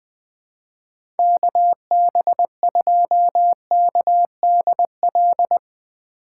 Morse code, heard as a keyed tone, about 21 wpm, 700 Hz. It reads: KB2KDL
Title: USA Call Signs - 6 Characters - 6x Character Spacing - 20wpm